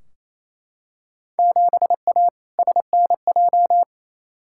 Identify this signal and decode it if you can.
Morse code, heard as a keyed tone, about 28 wpm, 700 Hz. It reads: 7A SNJ
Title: Field Day Exchanges Encore - 28wpm